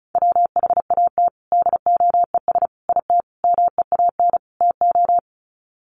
Morse code, heard as a keyed tone, about 34 wpm, 700 Hz. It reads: WHAT DOES IT MEAN TO
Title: Sets of 5 Words Encore2 35wpm